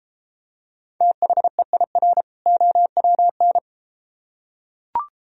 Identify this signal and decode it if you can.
Morse code, heard as a keyed tone, about 33 wpm, 700 Hz. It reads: THEIR OWN